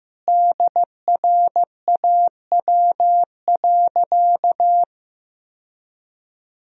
Morse code, heard as a keyed tone, about 15 wpm, 700 Hz. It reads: DRAW.
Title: Sentences from Top 1000 Words - Review Part 1 - 15wpm